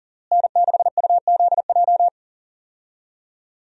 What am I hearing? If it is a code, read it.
Morse code, heard as a keyed tone, about 40 wpm, 700 Hz. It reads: N6UZJ
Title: USA Call Signs - 5 Characters - Repeated 3x - 40wpm